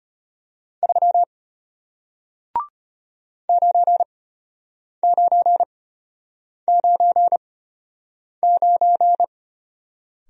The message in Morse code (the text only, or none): 3 9999